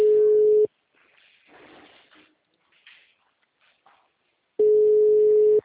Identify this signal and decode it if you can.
Ringback tone (Europe, 425 Hz).
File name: out- -2055